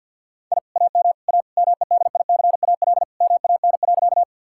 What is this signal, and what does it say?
Morse code, heard as a keyed tone, about 45 wpm, 700 Hz. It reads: I AM A MEDICAL MAN.
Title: Sentences from Top 1000 Words - Review Part 5 - 50wpm